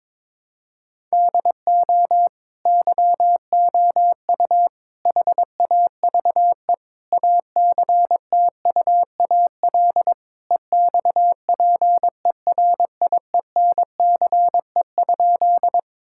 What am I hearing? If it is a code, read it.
Morse code, heard as a keyed tone, about 21 wpm, 700 Hz. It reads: DO YOU HAVE ACTUAL EXPERIENCE?
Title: Sentences from Top 1000 Words - Review Part 5 - 22wpm